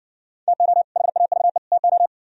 Morse code, heard as a keyed tone, about 58 wpm, 700 Hz. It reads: TO HAVE TO